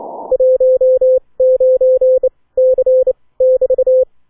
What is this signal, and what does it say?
Morse code, heard as a keyed tone, about 29 wpm, 520 Hz. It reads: 1 9 C <BT>